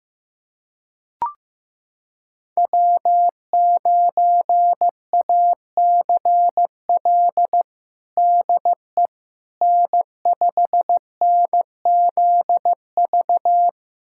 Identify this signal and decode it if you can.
Morse code, heard as a keyed tone, about 16 wpm, 700 Hz. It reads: W9ACL DE N5NZV
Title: USA Callsign DE Callsign - 5 Characters - 15wpm